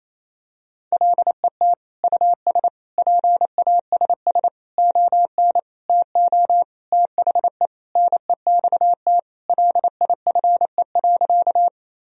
Morse code, heard as a keyed tone, about 27 wpm, 700 Hz. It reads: LET US PASS ON TO THE NEXT LIFE.
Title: Sentences from Top 1000 Words - Review Part 1 - 28wpm